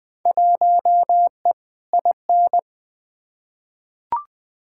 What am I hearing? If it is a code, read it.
Morse code, heard as a keyed tone, about 20 wpm, 700 Hz. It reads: 1E IN
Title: Field Day Exchanges - 20wpm